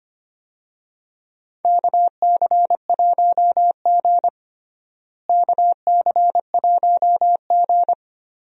Morse code, heard as a keyed tone, about 25 wpm, 700 Hz. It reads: KC1G KC1G
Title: USA Call Signs - 4 Characters - Repeated 3x - 25wpm